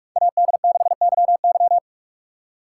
Morse code, heard as a keyed tone, about 42 wpm, 700 Hz. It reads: AD6YY